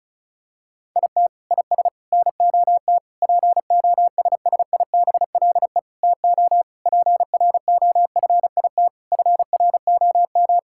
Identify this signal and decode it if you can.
Morse code, heard as a keyed tone, about 34 wpm, 700 Hz. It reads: IT IS NOT POSSIBLE TO PROFIT FROM
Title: Sentences from Top 1000 Words - Review Part 5 - 35wpm